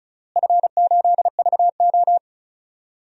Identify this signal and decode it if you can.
Morse code, heard as a keyed tone, about 34 wpm, 700 Hz. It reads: F8VO
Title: DXCC-100 Call Signs - 4 Characters - Repeated 3x - 35wpm